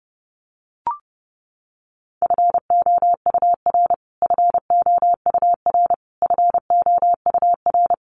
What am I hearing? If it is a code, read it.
Morse code, heard as a keyed tone, about 30 wpm, 700 Hz. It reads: FOUR FOUR FOUR
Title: Top 500 Words Repeated 30wpm